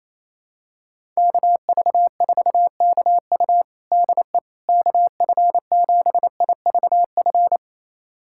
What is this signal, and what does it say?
Morse code, heard as a keyed tone, about 28 wpm, 700 Hz. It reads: KV4KU DE KF7IVF
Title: USA Callsign DE Callsign 28wpm